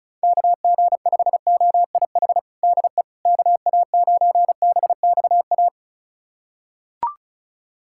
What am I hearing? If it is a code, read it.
Morse code, heard as a keyed tone, about 34 wpm, 690 Hz. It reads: KG5OIH DE KA9BXA
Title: USA Callsign DE Callsign - 6 Characters - 35wpm